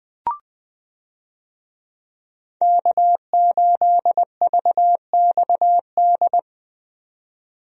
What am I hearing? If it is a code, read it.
Morse code, heard as a keyed tone, about 21 wpm, 700 Hz. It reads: K8VXD
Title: USA Call Signs - 5 Characters - 20wpm